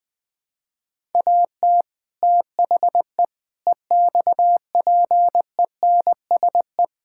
Morse code, heard as a keyed tone, about 20 wpm, 700 Hz. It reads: AT THE EXPENSE
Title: Build A Phrase - 4-Word Phrases - 20wpm